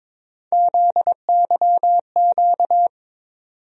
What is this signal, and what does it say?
Morse code, heard as a keyed tone, about 22 wpm, 700 Hz. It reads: ZYQ
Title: USA Call Sign Postfixes - 3 Letters 22wpm